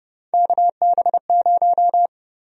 Morse code, heard as a keyed tone, about 30 wpm, 700 Hz. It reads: KB0
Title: USA Call Sign Prefixes - 30wpm